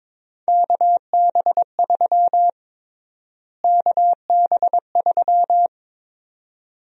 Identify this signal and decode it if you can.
Morse code, heard as a keyed tone, about 22 wpm, 700 Hz. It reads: KB3 KB3
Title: USA Call Sign Prefixes - Repeated 2x - 22wpm